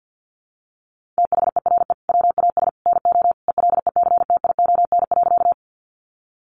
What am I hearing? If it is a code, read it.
Morse code, heard as a keyed tone, about 50 wpm, 700 Hz. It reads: THERE WAS NO ELECTION.
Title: Sentences from Top 1500 Words 50wpm